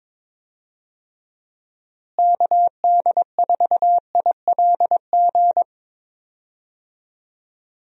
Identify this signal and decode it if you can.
Morse code, heard as a keyed tone, about 22 wpm, 700 Hz. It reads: KD4ILG
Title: USA Call Signs - 6 Characters - Repeated 2x - 22wpm